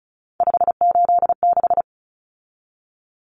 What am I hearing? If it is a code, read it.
Morse code, heard as a keyed tone, about 34 wpm, 700 Hz. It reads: 586
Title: Numbers - 3-Digit 35wpm